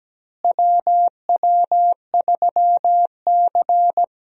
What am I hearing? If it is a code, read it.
Morse code, heard as a keyed tone, about 17 wpm, 700 Hz. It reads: WW3C
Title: USA Call Signs - 4 Characters - Repeated 3x - 17wpm